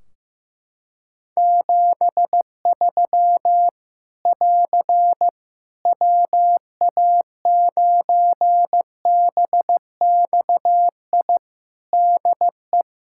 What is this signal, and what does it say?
Morse code, heard as a keyed tone, about 15 wpm, 700 Hz. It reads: 73 <AR> WA9BXI DE
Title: Practice QSO - Exchange 4 - Ending - 15wpm